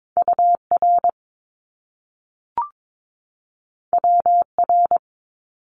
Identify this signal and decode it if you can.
Morse code, heard as a keyed tone, about 22 wpm, 700 Hz. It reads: UR WR
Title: Top Two Letter Combinations in Words 22wpm